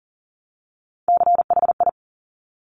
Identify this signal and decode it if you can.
Morse code, heard as a keyed tone, about 39 wpm, 700 Hz. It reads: CHI
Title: Top Three Letter Combinations in Words 40wpm